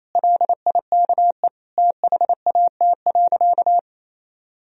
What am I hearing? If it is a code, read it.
Morse code, heard as a keyed tone, about 28 wpm, 700 Hz. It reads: LIKE THAT.
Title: Sentences from Top 1000 Words - Review Part 4 - 28wpm